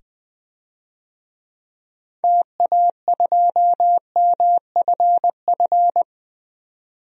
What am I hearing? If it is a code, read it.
Morse code, heard as a keyed tone, about 20 wpm, 700 Hz. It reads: TA2MFF
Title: DXCC-100 Call Signs - 6 Characters - 20wpm